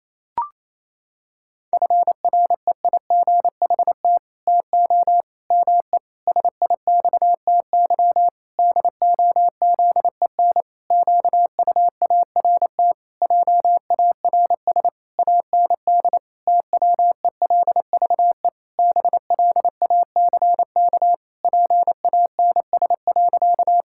Morse code, heard as a keyed tone, about 28 wpm, 700 Hz. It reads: FREIGHT TO ME SIXTY DOZEN QUART JARS AND TWELVE BLACK PANS.
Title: Pangram Challenge v2 - 28wpm